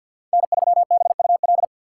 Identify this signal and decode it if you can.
Morse code, heard as a keyed tone, about 52 wpm, 700 Hz. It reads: N3BUL
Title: USA Call Signs - 5 Characters - 2x Character Spacing - 50wpm